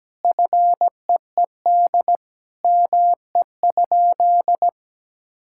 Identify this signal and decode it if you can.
Morse code, heard as a keyed tone, about 17 wpm, 700 Hz. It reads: FEED ME?
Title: Sentences from Top 1600 Words 17wpm